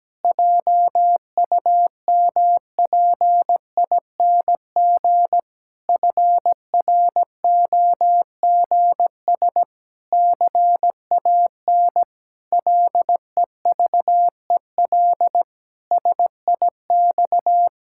Morse code, heard as a keyed tone, about 17 wpm, 700 Hz. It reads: JUMPING FROGS CAN LEVEL SIX
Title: Pangram Challenge v2 - 17wpm